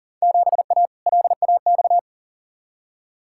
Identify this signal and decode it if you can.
Morse code, heard as a keyed tone, about 40 wpm, 700 Hz. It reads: 7A LAX
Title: Field Day Exchanges - 40wpm